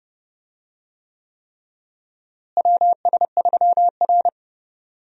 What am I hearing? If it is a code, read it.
Morse code, heard as a keyed tone, about 30 wpm, 700 Hz. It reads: WS3R
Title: USA Call Signs - 4 Characters - Repeated 3x - 30wpm